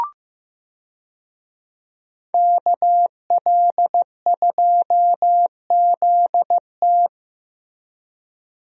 Morse code, heard as a keyed tone, about 15 wpm, 700 Hz. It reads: KL2ZT
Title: USA Call Signs - 5 Characters - Repeated 2x - 15wpm